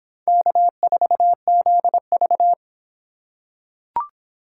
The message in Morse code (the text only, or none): K4ZV